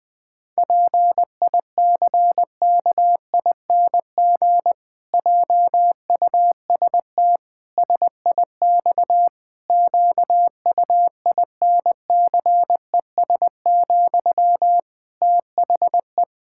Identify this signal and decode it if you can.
Morse code, heard as a keyed tone, about 21 wpm, 700 Hz. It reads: PICKING JUST SIX QUINCES, THE
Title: Pangram Challenge v2 - 20wpm